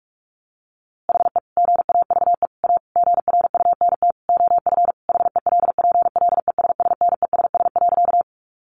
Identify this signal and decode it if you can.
Morse code, heard as a keyed tone, about 44 wpm, 700 Hz. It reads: HE GAVE A GRUNT OF HELPLESSNESS.
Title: A Sound Of Thunder 45wpm